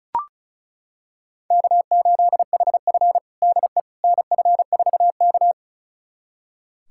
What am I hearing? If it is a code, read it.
Morse code, heard as a keyed tone, about 36 wpm, 700 Hz. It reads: K8HF DE NF4K
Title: USA Callsign DE Callsign - 4 Characters - 35wpm